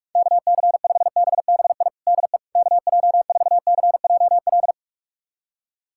Morse code, heard as a keyed tone, about 45 wpm, 700 Hz. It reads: KC5BBI DE KJ4CJL
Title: USA Callsign DE Callsign - 6 Characters - 45wpm